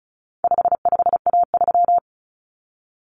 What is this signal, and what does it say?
Morse code, heard as a keyed tone, about 34 wpm, 700 Hz. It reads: 55A3
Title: Common Cut Numbers - 4-Digit 35wpm